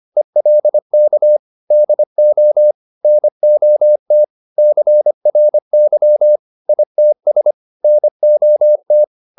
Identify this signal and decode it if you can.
Morse code, heard as a keyed tone, about 25 wpm, 600 Hz. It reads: ELK DO NOT CRY ITS NOT